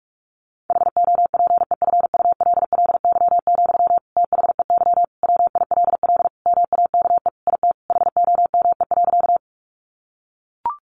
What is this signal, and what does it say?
Morse code, heard as a keyed tone, about 44 wpm, 700 Hz. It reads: HOPEFULLY, THEY WILL MAKE IT HOME.
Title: Sentences from Top 1500 Words 45wpm